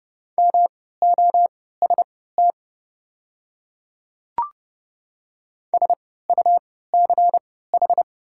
Morse code, heard as a keyed tone, about 30 wpm, spaced as for 17 wpm, 700 Hz. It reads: MOST SUCH
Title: 3x Character Spacing - Top 200 Words - 30wpm